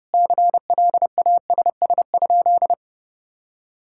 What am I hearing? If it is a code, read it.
Morse code, heard as a keyed tone, about 32 wpm, 700 Hz. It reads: CLASS?
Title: Sentences from Top 1000 Words - Review Part 5 - 30wpm